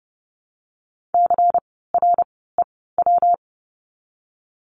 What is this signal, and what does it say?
Morse code, heard as a keyed tone, about 30 wpm, 700 Hz. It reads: C R E W
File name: course / 30:10wpm / 115 Lesson 19f - Letter w - Words 30wpm